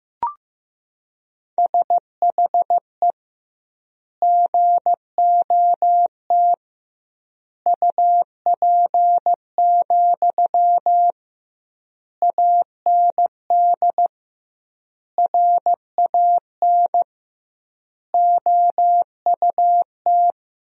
Morse code, heard as a keyed tone, about 15 wpm, 700 Hz. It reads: SHE GOT UP, AND RAN OUT
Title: 2x Word-Spacing - Sentences from Top 500 Words - 15wpm